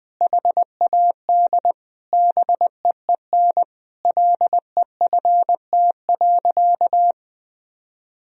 Morse code, heard as a keyed tone, about 20 wpm, 700 Hz. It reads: HAD BEEN LEFT.
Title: Sentences from Top 1000 Words - Review Part 2 - 20wpm